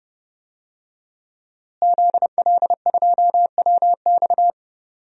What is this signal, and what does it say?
Morse code, heard as a keyed tone, about 30 wpm, 700 Hz. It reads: ZL2WX
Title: DXCC-100 Call Signs - 5 Characters - Repeated 2x - 30wpm